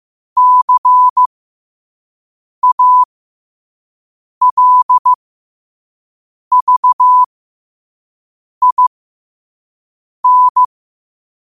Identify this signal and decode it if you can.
Morse code, heard as a keyed tone, about 15 wpm, spaced as for 5 wpm, 1000 Hz. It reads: CALVIN
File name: qotdfiles / 5wpm